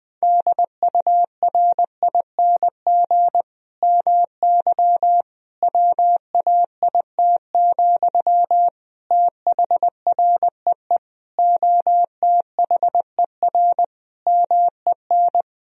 Morse code, heard as a keyed tone, about 20 wpm, 700 Hz. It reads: DURING MY WAIT, THREE OTHER MEN
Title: Sentences from Top 1000 Words - Review Part 2 - 20wpm